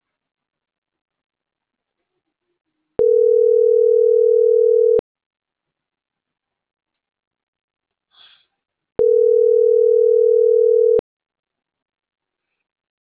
Ringback tone (North America), 440 + 480 Hz.